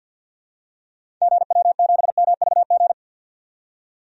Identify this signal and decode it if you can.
Morse code, heard as a keyed tone, about 51 wpm, 700 Hz. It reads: GW7MVG
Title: DXCC-100 Call Signs - 6 Characters - 50wpm